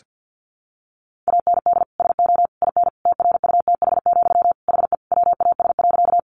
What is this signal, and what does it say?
Morse code, heard as a keyed tone, about 50 wpm, 700 Hz. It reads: AND SO IN TRUTH, HE WAS.
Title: Sentences from Top 800 Words 50wpm